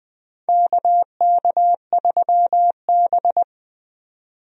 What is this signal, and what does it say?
Morse code, heard as a keyed tone, about 20 wpm, 700 Hz. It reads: KK3B
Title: USA Call Signs - 4 Characters - 20wpm